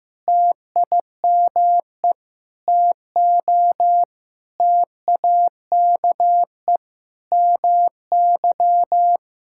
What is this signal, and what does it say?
Morse code, heard as a keyed tone, about 15 wpm, 700 Hz. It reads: TIME TO TAKE MY
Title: Sentences from Top 1000 Words - Review Part 5 - 15wpm